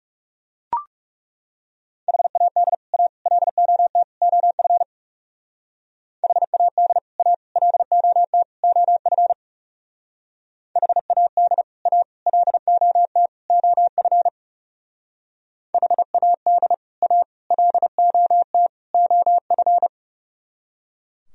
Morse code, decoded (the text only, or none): HAD A LOT OF HAD A LOT OF HAD A LOT OF HAD A LOT OF